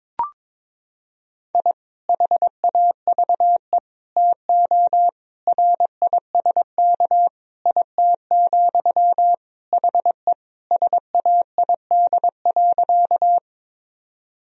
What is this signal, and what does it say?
Morse code, heard as a keyed tone, about 22 wpm, 700 Hz. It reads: I HAVE TO RISK IT, HE SAID.
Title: Sentences from Top 1000 Words - Review Part 4 - 22wpm